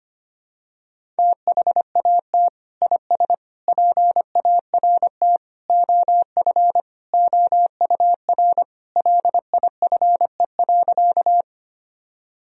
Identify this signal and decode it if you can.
Morse code, heard as a keyed tone, about 25 wpm, 700 Hz. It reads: THAT IS PART OF OUR LIFE.
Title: Sentences from Top 1000 Words - Review Part 1 - 25wpm